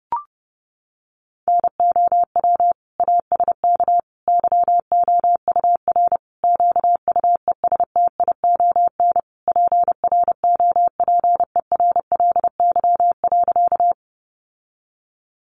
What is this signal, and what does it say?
Morse code, heard as a keyed tone, about 30 wpm, 700 Hz. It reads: NOW ASK YOUR QUESTION PROPERLY.
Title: Sentences from Top 1100 Words 30wpm